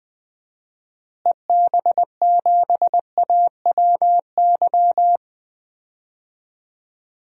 Morse code, heard as a keyed tone, about 20 wpm, 700 Hz. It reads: EB7AWY
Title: DXCC-100 Call Signs - 6 Characters - Repeated 2x - 20wpm